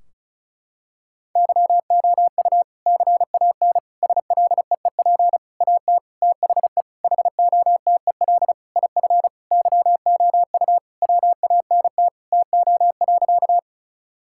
Morse code, heard as a keyed tone, about 36 wpm, 700 Hz. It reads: YOU CAN SLEEP AT THE HOTEL IF YOU WANT TO.
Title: Sentences from Top 1600 Words 35wpm